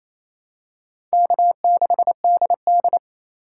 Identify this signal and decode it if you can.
Morse code, heard as a keyed tone, about 28 wpm, 700 Hz. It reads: K6DD
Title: USA Call Signs - 4 Characters - 3x Character Spacing - 28wpm